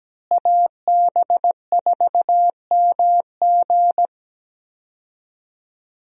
Morse code, heard as a keyed tone, about 17 wpm, 700 Hz. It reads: AB4MG
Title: USA Call Signs - 5 Characters - Repeated 2x - 17wpm